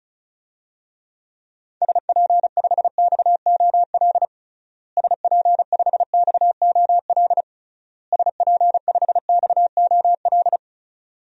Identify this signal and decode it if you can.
Morse code, heard as a keyed tone, about 36 wpm, 700 Hz. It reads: SP5XOL SP5XOL SP5XOL
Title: DXCC-100 Call Signs - 6 Characters - Repeated 3x - 35wpm